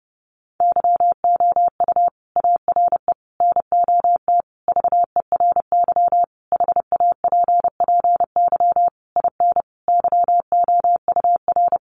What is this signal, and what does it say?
Morse code, heard as a keyed tone, about 28 wpm, 700 Hz. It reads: YOU ARE NOT VERY HAPPY IN YOUR
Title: Sentences from Top 1200 Words 30wpm